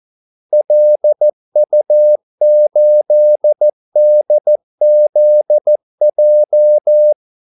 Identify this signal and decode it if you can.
Morse code, heard as a keyed tone, about 14 wpm, 600 Hz. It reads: LU8DZJ
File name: Callsigns-14wpm